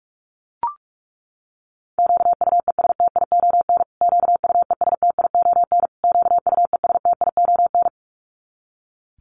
Morse code, heard as a keyed tone, about 44 wpm, 700 Hz. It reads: QUESTION QUESTION QUESTION
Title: Top 500 Words Repeated 45wpm